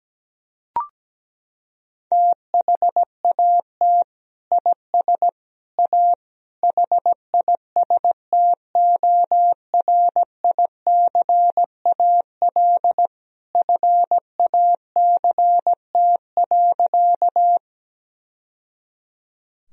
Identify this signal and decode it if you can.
Morse code, heard as a keyed tone, about 17 wpm, 700 Hz. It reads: THAT IS A HISTORICAL FACT.
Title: Sentences from Top 1000 Words - Review Part 4 - 17wpm